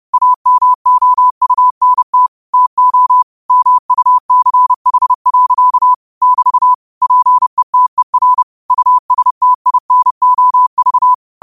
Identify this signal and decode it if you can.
Morse code, heard as a keyed tone, about 30 wpm, 1000 Hz. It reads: AMOUNT TO MUCH. <BT> PETER USTINOV
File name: qotdfiles / 30wpm